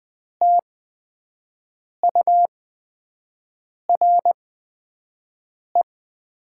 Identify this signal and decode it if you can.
Morse code, heard as a keyed tone, about 20 wpm, spaced as for 5 wpm, 700 Hz. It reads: TURE
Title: 8x Character Spacing - Top 500 Words - 20wpm